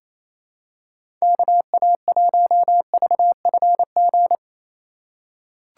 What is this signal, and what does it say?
Morse code, heard as a keyed tone, about 28 wpm, 700 Hz. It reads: KA1VFG
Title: USA Call Signs - 6 Characters - 6x Character Spacing - 28wpm